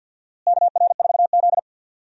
Morse code, heard as a keyed tone, about 52 wpm, 700 Hz. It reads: KR4Z
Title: USA Call Signs - 4 Characters - Repeated 3x - 50wpm